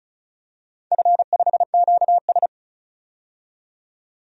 Morse code, heard as a keyed tone, about 34 wpm, 700 Hz. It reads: F5QS
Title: DXCC-100 Call Signs - 4 Characters - 35wpm